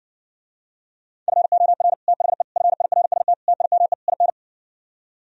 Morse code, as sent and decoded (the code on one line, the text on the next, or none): ..-. --- .-. / - .... / ..-. .. .-. ... - / - .. -- . / .. -.
FOR THE FIRST TIME IN